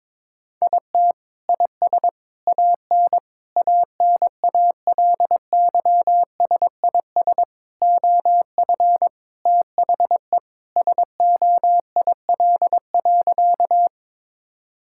Morse code, heard as a keyed tone, about 21 wpm, 700 Hz. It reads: IT IS AN ANALYSIS OF THE SOIL.
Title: Sentences from Top 1000 Words - Review Part 5 - 22wpm